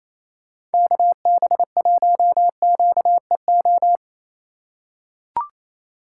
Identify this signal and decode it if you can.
Morse code, heard as a keyed tone, about 27 wpm, 700 Hz. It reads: KB1QEO
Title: USA Call Signs - 6 Characters - 3x Character Spacing - 28wpm